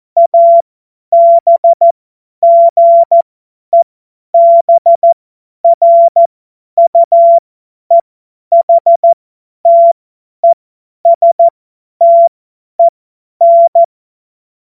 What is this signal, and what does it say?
Morse code, heard as a keyed tone, about 14 wpm, spaced as for 10 wpm, 680 Hz. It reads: ABGEBRUEHTESTEN
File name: Morsekurs / Witze / Witz 02. August 2022 50 F70